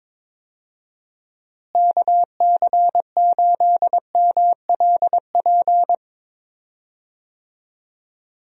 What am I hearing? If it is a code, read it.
Morse code, heard as a keyed tone, about 22 wpm, 700 Hz. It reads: KC8MLP
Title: USA Call Signs - 6 Characters - Repeated 3x - 22wpm